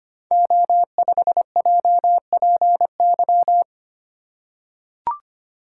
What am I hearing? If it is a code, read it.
Morse code, heard as a keyed tone, about 25 wpm, 700 Hz. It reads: O5JPY
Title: Code Groups - 5 Letter:Numbers - 25wpm